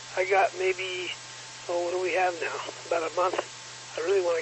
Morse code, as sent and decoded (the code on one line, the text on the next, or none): none